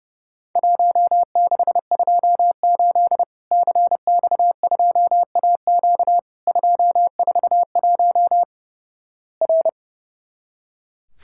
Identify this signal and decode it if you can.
Morse code, heard as a keyed tone, about 32 wpm, 700 Hz. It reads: 1628 CX2AQ 241 R
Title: Worked All Europe DX Contest - QTC Exchange - 30wpm